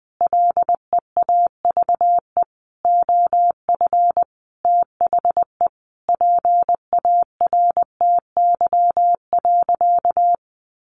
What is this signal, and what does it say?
Morse code, heard as a keyed tone, about 19 wpm, 700 Hz. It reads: LEAVE OF THE PARTY.
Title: Sentences from Top 1400 Words 20wpm